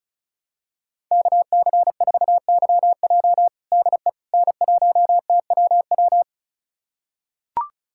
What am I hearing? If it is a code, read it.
Morse code, heard as a keyed tone, about 36 wpm, 690 Hz. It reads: KC4YJ DE N1TWW
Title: USA Callsign DE Callsign - 5 Characters - 35wpm